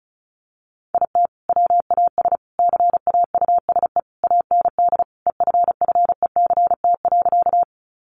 Morse code, heard as a keyed tone, about 34 wpm, 700 Hz. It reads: IT WAS CAUSE AND EFFECT.
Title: Sentences from Top 1000 Words 35wpm